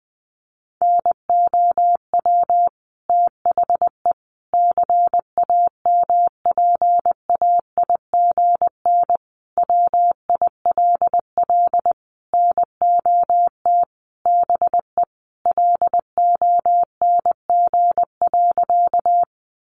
Morse code, heard as a keyed tone, about 20 wpm, 700 Hz. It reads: NOW THE CAMPAIGN WILL NOT BE LONG.
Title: Sentences from Top 1200 Words 20wpm